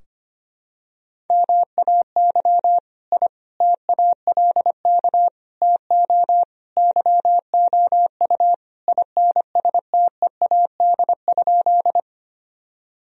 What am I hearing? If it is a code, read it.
Morse code, heard as a keyed tone, about 25 wpm, 700 Hz. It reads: MAY I TALK TO YOU INSTEAD?
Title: Sentences from Top 1000 Words - Review Part 4 - 25wpm